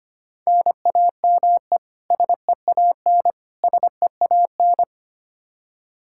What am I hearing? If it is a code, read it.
Morse code, heard as a keyed tone, about 25 wpm, 700 Hz. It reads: NAME SEAN SEAN
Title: Practice QSO - Exchange 1 - Beginning - 25wpm